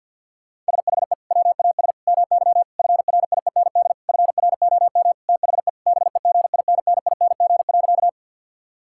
Morse code, heard as a keyed tone, about 50 wpm, 700 Hz. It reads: SHE WAS MY FRIEND FROM THE BEGINNING.
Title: Sentences from Top 1000 Words - Review Part 5 - 50wpm